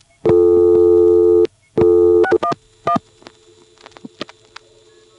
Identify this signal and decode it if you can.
DTMF, heard as telephone touch tones, tones of about 80 ms. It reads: 611